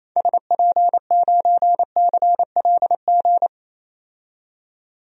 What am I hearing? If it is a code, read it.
Morse code, heard as a keyed tone, about 28 wpm, 700 Hz. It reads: SP9CLG
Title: DXCC-100 Call Signs - 6 Characters - 28wpm